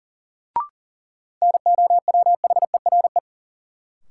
Morse code, heard as a keyed tone, about 40 wpm, 700 Hz. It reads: NOWHERE
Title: Sets of 5 Words Encore - New Word Drill - 40wpm